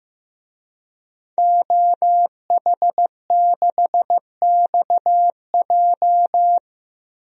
Morse code, heard as a keyed tone, about 15 wpm, 700 Hz. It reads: OH6XJ
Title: DXCC-100 Call Signs - 5 Characters - Repeated 2x - 15wpm